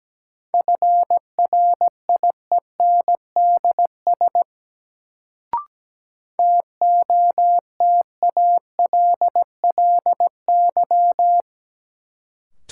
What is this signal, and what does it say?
Morse code, heard as a keyed tone, about 17 wpm, 700 Hz. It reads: FRIENDS TOTALLY
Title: Sentences from Top 1600 Words - New Word Drill Alt - 17wpm